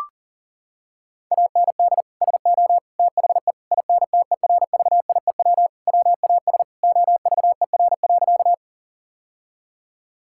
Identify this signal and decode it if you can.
Morse code, heard as a keyed tone, about 37 wpm, 700 Hz. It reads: AND SO THE INTERVIEW WAS OVER.
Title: Sentences from Top 1600 Words 40wpm